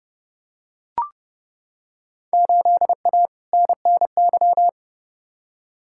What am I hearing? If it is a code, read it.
Morse code, heard as a keyed tone, about 30 wpm, 700 Hz. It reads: 8A NNY
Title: Field Day Exchanges Encore - 30wpm